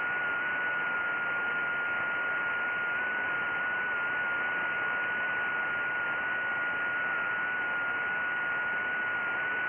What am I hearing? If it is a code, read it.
FT8.
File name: IC-7100